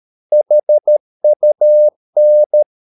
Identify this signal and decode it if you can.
Morse code, heard as a keyed tone, about 13 wpm, 600 Hz. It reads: HUN
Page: Ferie 13 wpm